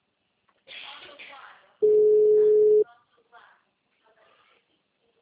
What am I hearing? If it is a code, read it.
Ringback tone (Europe, 425 Hz), with speech.
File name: out- -2056- - - .4544